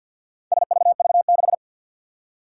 Morse code, heard as a keyed tone, about 47 wpm, 700 Hz. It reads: SVV6
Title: Four Challenging Letter-Number Combinations 50wpm